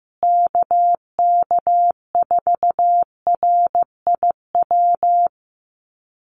Morse code, heard as a keyed tone, about 15 wpm, 700 Hz. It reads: KK4RIW
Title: USA Call Signs Repeated 3x 15wpm